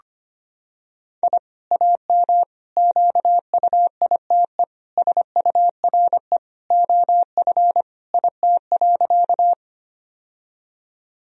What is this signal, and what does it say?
Morse code, heard as a keyed tone, about 25 wpm, 700 Hz. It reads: I AM QUITE SURE OF IT.
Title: Sentences from Top 1000 Words - Review Part 2 - 25wpm